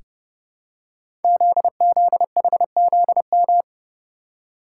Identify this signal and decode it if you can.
Morse code, heard as a keyed tone, about 30 wpm, 700 Hz. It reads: ZZHZM
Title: Code Groups - 5 Letters - 30wpm